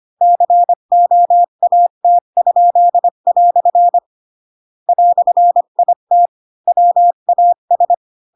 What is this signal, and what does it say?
Morse code, heard as a keyed tone, about 25 wpm, 700 Hz. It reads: COAT?" "IT WAS
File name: Pinocchio / Pinocchio ch8 20 25 ik1hge